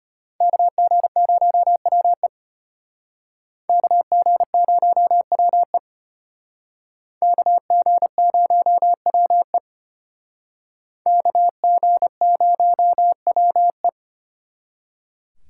Morse code, heard as a keyed tone, about 32 wpm, 700 Hz. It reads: KG0WE KG0WE KG0WE KG0WE